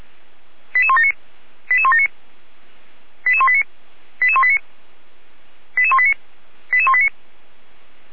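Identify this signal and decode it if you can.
Five-tone selective call (ZVEI), heard as quick runs of five tones, tones of about 70 ms. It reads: 88179 88179 88189 88189 88189 88189
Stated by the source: Received on SDR as ZVEI five-tone selective call 88169.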